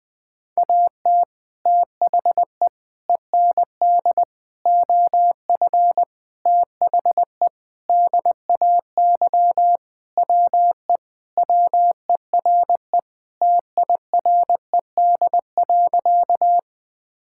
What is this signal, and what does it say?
Morse code, heard as a keyed tone, about 21 wpm, 700 Hz. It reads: AT THE END OF THE DAY WE WERE TIRED.
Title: Sentences with 5-Word Phrases - 20wpm